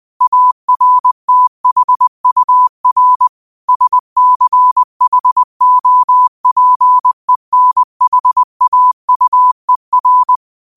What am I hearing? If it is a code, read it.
Morse code, heard as a keyed tone, about 20 wpm, 1000 Hz. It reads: ARTHUR SCHOPENHAUER